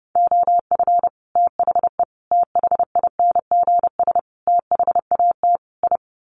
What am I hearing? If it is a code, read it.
Morse code, heard as a keyed tone, about 30 wpm, 700 Hz. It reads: OF THE THINGS THAT I
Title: Sets of 5 Words Encore 30wpm